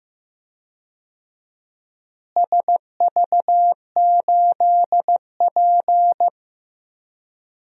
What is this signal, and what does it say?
Morse code, heard as a keyed tone, about 15 wpm, 700 Hz. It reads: SV8P
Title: DXCC-100 Call Signs - 4 Characters - Repeated 2x - 15wpm